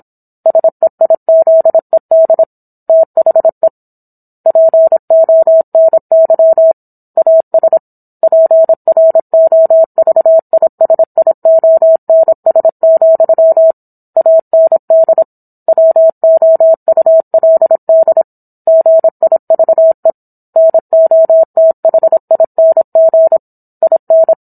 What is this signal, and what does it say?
Morse code, heard as a keyed tone, about 27 wpm, 650 Hz. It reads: SEIZED THE PONY AS PROVISIONS, AND WOULD GIVE NOTHING IN